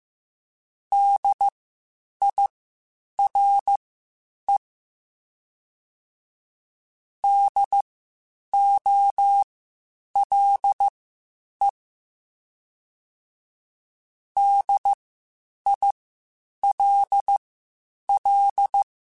Morse code, heard as a keyed tone, about 15 wpm, spaced as for 8 wpm, 800 Hz. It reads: DIRE DOLE DILL